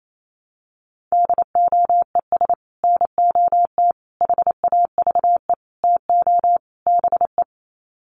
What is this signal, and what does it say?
Morse code, heard as a keyed tone, about 28 wpm, 700 Hz. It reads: DOES NOT HAVE TO BE
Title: Sets of 5 Words 28wpm